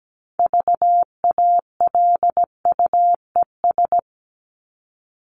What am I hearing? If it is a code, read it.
Morse code, heard as a keyed tone, about 17 wpm, 700 Hz. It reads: VALUES
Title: ES:S Suffix - Top 100 Words with Base Words 17wpm